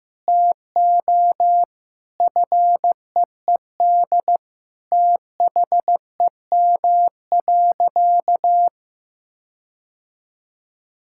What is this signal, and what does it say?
Morse code, heard as a keyed tone, about 16 wpm, 700 Hz. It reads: TO FEED THEM.
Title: Sentences from Top 1600 Words 15wpm